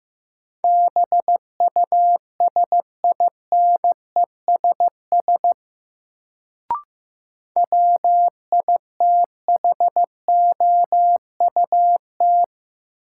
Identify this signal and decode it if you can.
Morse code, heard as a keyed tone, about 15 wpm, 700 Hz. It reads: BUSINESS WITHOUT
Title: Sentences from Top 600 Words - New Word Drill - 15wpm